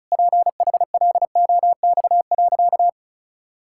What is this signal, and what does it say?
Morse code, heard as a keyed tone, about 34 wpm, 700 Hz. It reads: PHLOX.
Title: Pangram Challenge v2 - 35wpm